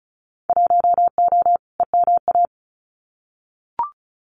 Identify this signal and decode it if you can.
Morse code, heard as a keyed tone, about 36 wpm, 700 Hz. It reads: 1O EMA
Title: Winter Field Day Exchanges 35wpm